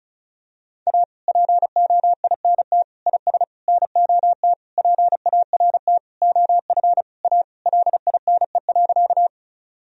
Morse code, heard as a keyed tone, about 36 wpm, 700 Hz. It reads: A POINT IS NOT PART OF A LINE.
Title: Sentences from Top 1000 Words - Review Part 1 - 35wpm